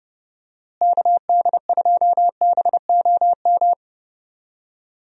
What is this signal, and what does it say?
Morse code, heard as a keyed tone, about 30 wpm, 700 Hz. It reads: KD2BOM
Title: USA Call Signs - 6 Characters - Repeated 3x - 30wpm